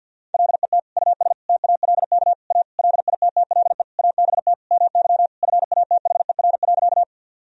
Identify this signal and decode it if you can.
Morse code, heard as a keyed tone, about 51 wpm, 700 Hz. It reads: LET US TALK A LITTLE ABT MY FATHER.
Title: Sentences from Top 400 Words and Abbreviations 50wpm